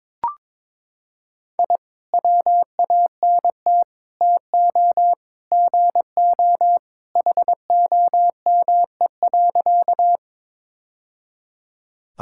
Morse code, heard as a keyed tone, about 22 wpm, 700 Hz. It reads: I WANT TO GO HOME.
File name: practice / 22wpm / Sentences with 3-Word Phrases - 22wpm